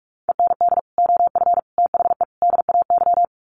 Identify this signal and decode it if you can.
Morse code, heard as a keyed tone, about 44 wpm, 700 Hz. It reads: END OF THE DAY
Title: Sets of 4 Words 45wpm